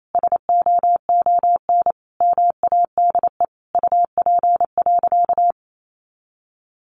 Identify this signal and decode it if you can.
Morse code, heard as a keyed tone, about 28 wpm, 700 Hz. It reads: SOON MADE UP.
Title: Sentences from Top 500 Words 28wpm